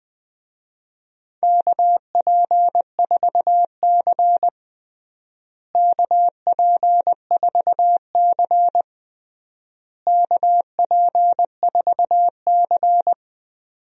Morse code, heard as a keyed tone, about 19 wpm, 700 Hz. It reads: KP4C KP4C KP4C
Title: DXCC-100 Call Signs - 4 Characters - Repeated 3x - 20wpm